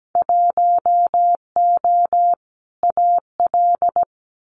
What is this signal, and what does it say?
Morse code, heard as a keyed tone, about 17 wpm, 700 Hz. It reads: 1O AL
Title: Winter Field Day Exchanges 17wpm